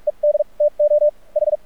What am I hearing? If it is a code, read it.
Morse code, heard as a keyed tone, about 46 wpm, spaced as for 37 wpm, 590 Hz. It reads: ED TO H
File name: archived / CRSnet